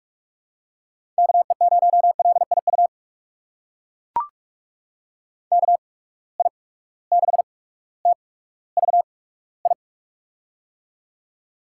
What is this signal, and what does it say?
Morse code, heard as a keyed tone, about 44 wpm, spaced as for 19 wpm, 700 Hz. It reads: KE0LIU KI6TVI
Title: USA Call Signs - 8x Character Spacing - 45wpm